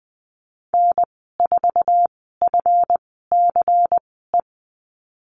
Morse code, heard as a keyed tone, about 20 wpm, spaced as for 15 wpm, 700 Hz. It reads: N4FCE
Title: 154 Lesson 25g - Letter p - Call Signs 20wpm